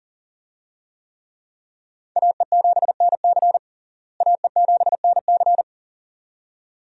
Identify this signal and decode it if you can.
Morse code, heard as a keyed tone, about 40 wpm, 700 Hz. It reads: AE7NC AE7NC
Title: USA Call Signs - 5 Characters - Repeated 2x - 40wpm